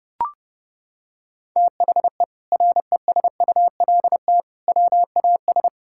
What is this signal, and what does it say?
Morse code, heard as a keyed tone, about 30 wpm, 700 Hz. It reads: THE RESULT WAS
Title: Sentences from Top 1000 Words - Review Part 6 - 30wpm